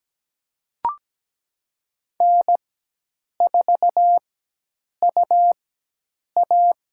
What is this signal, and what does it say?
Morse code, heard as a keyed tone, about 17 wpm, spaced as for 8 wpm, 700 Hz. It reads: N4UA
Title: USA Call Signs - 4 Characters - 4x Character Spacing - 17wpm